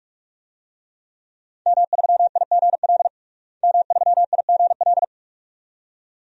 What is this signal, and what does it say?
Morse code, heard as a keyed tone, about 43 wpm, 700 Hz. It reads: M3IGL M3IGL
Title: DXCC-100 Call Signs - 5 Characters - Repeated 2x - 45wpm